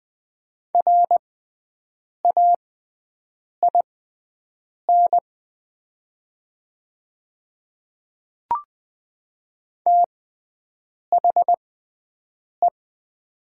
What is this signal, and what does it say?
Morse code, heard as a keyed tone, about 20 wpm, spaced as for 7 wpm, 700 Hz. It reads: RAIN THE
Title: 6x Character Spacing - Top 300 Words - 20wpm